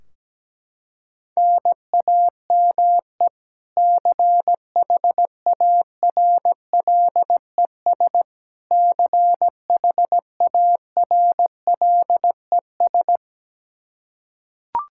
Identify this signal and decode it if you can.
Morse code, heard as a keyed tone, about 17 wpm, 700 Hz. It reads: NAME CHARLES CHARLES
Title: Practice QSO - Exchange 1 - Beginning - 17wpm